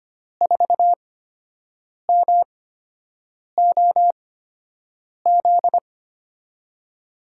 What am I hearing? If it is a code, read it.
Morse code, heard as a keyed tone, about 25 wpm, spaced as for 7 wpm, 700 Hz. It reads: 4MOZ
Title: USA Call Signs - 6 Characters - 8x Character Spacing - 25wpm